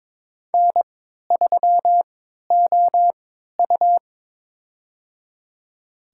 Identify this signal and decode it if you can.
Morse code, heard as a keyed tone, about 23 wpm, spaced as for 13 wpm, 700 Hz. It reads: N3OU
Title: USA Call Signs - 3x Character Spacing - 22wpm